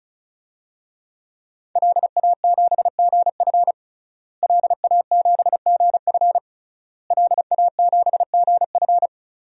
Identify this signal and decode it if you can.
Morse code, heard as a keyed tone, about 34 wpm, 700 Hz. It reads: LA7GF LA7GF LA7GF
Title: DXCC-100 Call Signs - 5 Characters - Repeated 3x - 35wpm